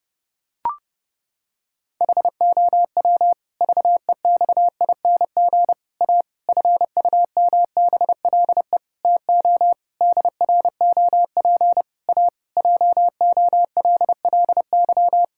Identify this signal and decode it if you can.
Morse code, heard as a keyed tone, about 30 wpm, 700 Hz. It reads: HOW VEXING A FUMBLE TO DROP A JOLLY
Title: Pangram Challenge v2 - 30wpm